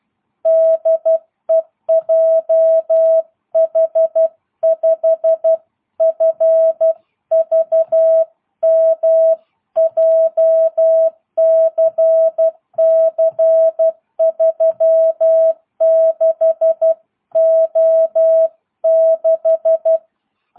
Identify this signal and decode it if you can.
Morse code, heard as a keyed tone, about 12 wpm, 650 Hz. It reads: DE JH5FVM JCC36O6
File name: jh8cbh / 2022cw